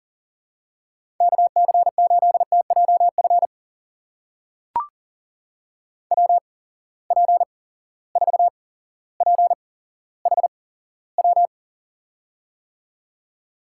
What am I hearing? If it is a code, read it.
Morse code, heard as a keyed tone, about 39 wpm, spaced as for 17 wpm, 700 Hz. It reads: KC8TJF WP4PHW
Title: USA Call Signs - 6 Characters - 8x Character Spacing - 40wpm